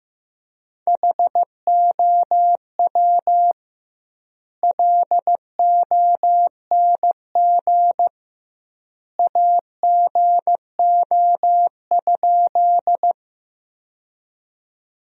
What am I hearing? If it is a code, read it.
Morse code, heard as a keyed tone, about 15 wpm, 700 Hz. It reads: HOW LONG AGO?
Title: 2x Word-Spacing - Sentences from Top 500 Words - 15wpm